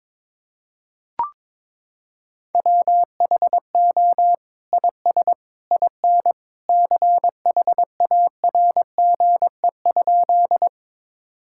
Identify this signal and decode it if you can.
Morse code, heard as a keyed tone, about 22 wpm, 700 Hz. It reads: WHO IS IN CHARGE?
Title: Sentences from Top 1000 Words - Review Part 6 - 22wpm